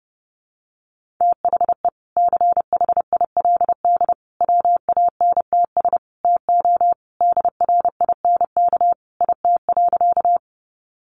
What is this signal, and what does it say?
Morse code, heard as a keyed tone, about 30 wpm, 700 Hz. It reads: THE CHILD WANTS TO DRINK IT.
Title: Additional Easy Phrases 30wpm